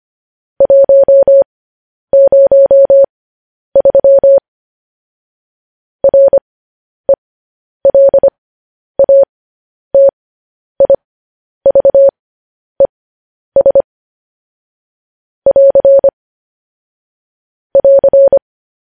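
Morse code, heard as a keyed tone, about 25 wpm, spaced as for 10 wpm, 550 Hz. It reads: 103 RELATIVES <AR> <AR>
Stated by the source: Morse code news headlines broadcast.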